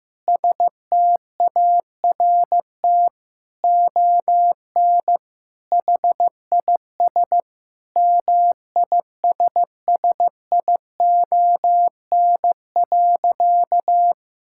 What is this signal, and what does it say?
Morse code, heard as a keyed tone, about 14 wpm, 700 Hz. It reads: START ON HIS MISSION.
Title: Sentences from Top 1600 Words 15wpm